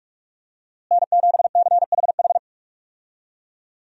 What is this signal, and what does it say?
Morse code, heard as a keyed tone, about 44 wpm, 700 Hz. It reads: N7CHH
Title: USA Call Signs - 5 Characters - Repeated 2x - 45wpm